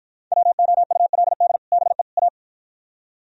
Morse code, heard as a keyed tone, about 52 wpm, 700 Hz. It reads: WOULD BE A